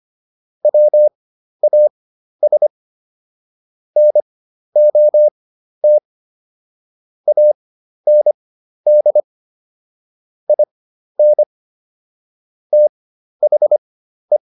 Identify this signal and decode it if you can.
Morse code, heard as a keyed tone, about 25 wpm, spaced as for 12 wpm, 600 Hz. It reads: WAS NOT AND IN THE